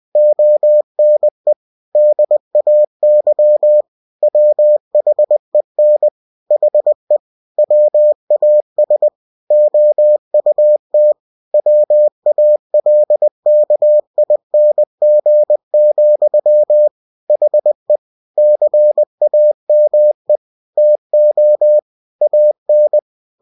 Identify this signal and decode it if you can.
Morse code, heard as a keyed tone, about 20 wpm, 600 Hz. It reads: ONE DAY WHEN HE WAS OUT WALKING, HE CAME TO AN
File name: cw-books / winnie-the-pooh / winnie-20wpm-page-0002